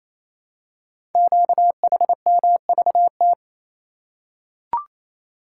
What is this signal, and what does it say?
Morse code, heard as a keyed tone, about 29 wpm, 700 Hz. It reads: QHMVT
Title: Code Groups - 5 Letters - 28wpm